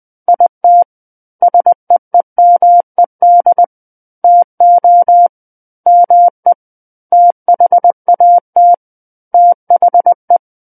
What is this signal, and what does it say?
Morse code, heard as a keyed tone, about 20 wpm, 700 Hz. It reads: IT SEEMED TO ME THAT THE
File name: 11-Chapter 11 - 20 WPM